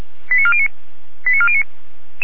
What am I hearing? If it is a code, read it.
Five-tone selective call (ZVEI), heard as quick runs of five tones, tones of about 70 ms. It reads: 89409 89409